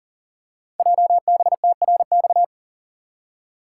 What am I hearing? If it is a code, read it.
Morse code, heard as a keyed tone, about 42 wpm, 690 Hz. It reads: JBTRX
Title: Code Groups - 5 Letters - 40wpm